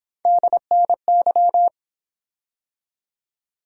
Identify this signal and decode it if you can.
Morse code, heard as a keyed tone, about 29 wpm, 700 Hz. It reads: DNY